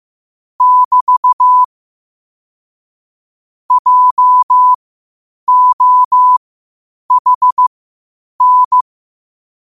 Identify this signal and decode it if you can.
Morse code, heard as a keyed tone, about 15 wpm, spaced as for 8 wpm, 1000 Hz. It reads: <BT> JOHN
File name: qotdfiles / 8wpm